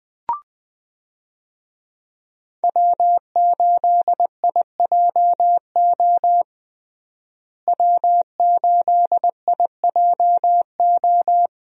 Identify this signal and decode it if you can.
Morse code, heard as a keyed tone, about 20 wpm, 700 Hz. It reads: W8IJO W8IJO
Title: USA Call Signs - 5 Characters - Repeated 3x - 20wpm